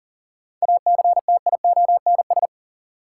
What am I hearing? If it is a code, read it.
Morse code, heard as a keyed tone, about 39 wpm, 700 Hz. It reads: ACTIONS
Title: Sentences from Top 1000 Words - Review Part 6 - New Word Drill - 40wpm